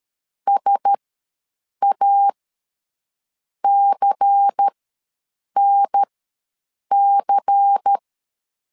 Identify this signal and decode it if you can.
Morse code, heard as a keyed tone, about 13 wpm, spaced as for 7 wpm, 800 Hz. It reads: SACNC
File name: LEZIONI / 5 LBVC / 5 LBVC ws 12wpm 3159